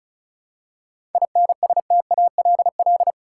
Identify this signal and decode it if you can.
Morse code, heard as a keyed tone, about 36 wpm, 700 Hz. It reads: INSTALL
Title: Seven Letter Words - 35wpm